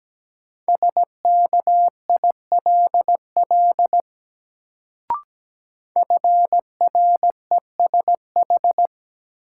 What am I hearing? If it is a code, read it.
Morse code, heard as a keyed tone, about 17 wpm, 700 Hz. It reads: SKILL FRESH
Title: Sentences from Top 1200 Words - New Word Drill Alt - 17wpm